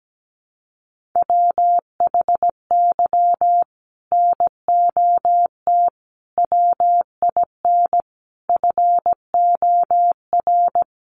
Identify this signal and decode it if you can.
Morse code, heard as a keyed tone, about 17 wpm, 700 Hz. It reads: WHY NOT WIN FOR
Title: Sentences from Top 1500 Words 17wpm